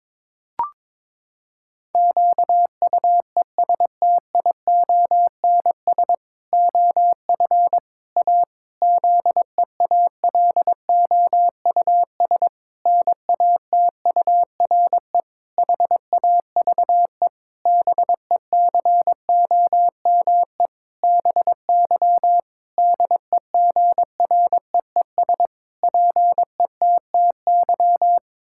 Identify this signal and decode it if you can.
Morse code, heard as a keyed tone, about 22 wpm, 700 Hz. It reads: QUESTIONS OF A ZEALOUS NATURE HAVE BECOME BY DEGREES PETTY
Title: Pangram Challenge v2 - 22wpm